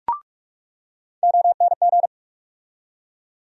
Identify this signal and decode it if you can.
Morse code, heard as a keyed tone, about 42 wpm, 700 Hz. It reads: ONG
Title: USA Call Sign Postfixes - 3 Letters 45wpm